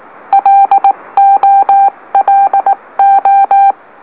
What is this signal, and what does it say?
Morse code, heard as a keyed tone, about 18 wpm, 790 Hz. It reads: LOLO
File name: M04- -2345utc-LOLO